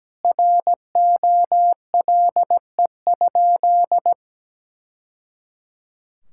Morse code, heard as a keyed tone, about 17 wpm, 700 Hz. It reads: ROLE?
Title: Sentences from Top 1000 Words - Review Part 6 - 17wpm